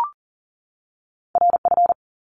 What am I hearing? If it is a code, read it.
Morse code, heard as a keyed tone, about 40 wpm, 700 Hz. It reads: RF
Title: Top Two Letter Combinations in Words 40wpm